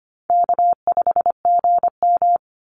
Morse code, heard as a keyed tone, about 25 wpm, 700 Hz. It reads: K5GM
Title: Easy Call Signs Repeated 3x - 25wpm